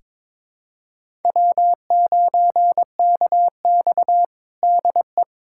Morse code, heard as a keyed tone, about 22 wpm, 700 Hz. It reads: W9KX DE
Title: USA Callsign DE Callsign - 4 Characters - 22wpm